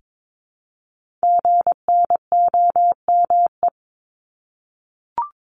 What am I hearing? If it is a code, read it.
Morse code, heard as a keyed tone, about 22 wpm, 700 Hz. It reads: GNOME
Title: Multi-letter Phonemes Repeated 3x 22wpm